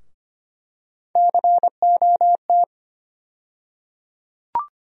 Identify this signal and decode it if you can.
Morse code, heard as a keyed tone, about 25 wpm, 700 Hz. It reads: COT
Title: USA Call Sign Postfixes - 3 Letters 25wpm